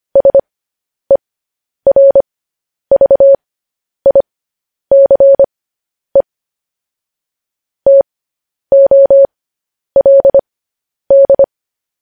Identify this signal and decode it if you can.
Morse code, heard as a keyed tone, about 25 wpm, spaced as for 10 wpm, 550 Hz. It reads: SERVICE TOLD
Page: news-headlines-in-morse-code-at-10-wpm